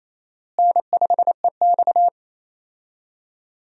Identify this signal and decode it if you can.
Morse code, heard as a keyed tone, about 28 wpm, 700 Hz. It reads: N5EX
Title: USA Call Signs - 4 Characters - Repeated 2x - 28wpm